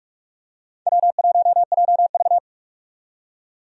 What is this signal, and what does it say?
Morse code, heard as a keyed tone, about 46 wpm, 700 Hz. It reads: W1JV
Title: USA Call Signs - 4 Characters - 45wpm